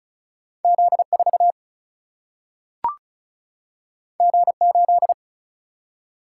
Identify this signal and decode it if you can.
Morse code, heard as a keyed tone, about 36 wpm, 700 Hz. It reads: Z4 G8
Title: Single-Letter Single-Number 35wpm